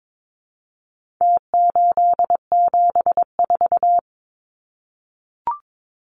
Morse code, heard as a keyed tone, about 22 wpm, 700 Hz. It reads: T874
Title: Numbers - Contest Style - 4-Digit 22wpm